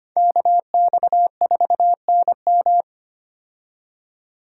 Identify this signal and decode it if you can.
Morse code, heard as a keyed tone, about 25 wpm, 700 Hz. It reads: KX4NM
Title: K1USN SST QSO Practice - 25wpm